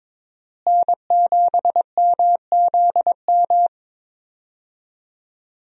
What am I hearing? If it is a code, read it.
Morse code, heard as a keyed tone, about 22 wpm, 700 Hz. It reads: N7MZM
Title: USA Call Signs - 5 Characters - Repeated 3x - 22wpm